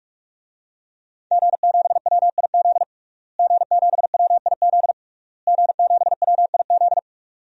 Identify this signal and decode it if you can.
Morse code, heard as a keyed tone, about 46 wpm, 700 Hz. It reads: G7WIZ G7WIZ G7WIZ
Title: DXCC-100 Call Signs - 5 Characters - Repeated 3x - 45wpm